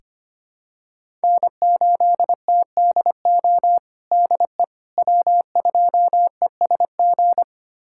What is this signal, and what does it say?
Morse code, heard as a keyed tone, about 25 wpm, 700 Hz. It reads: N8TDO DE W2ESG
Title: USA Callsign DE Callsign - 5 Characters - 25wpm